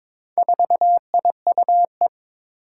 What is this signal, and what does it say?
Morse code, heard as a keyed tone, about 22 wpm, 700 Hz. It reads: 4IUE
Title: Four Challenging Letter-Number Combinations 22wpm